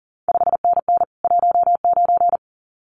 Morse code, heard as a keyed tone, about 37 wpm, 700 Hz. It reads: 5NN 19
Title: CQ WW DX Contest Exchange Encore - 40wpm